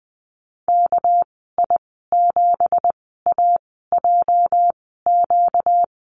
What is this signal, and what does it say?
Morse code, heard as a keyed tone, about 20 wpm, spaced as for 14 wpm, 700 Hz. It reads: KI7AJQ
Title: 245 Lesson 39g - Letter z - Call Signs 20wpm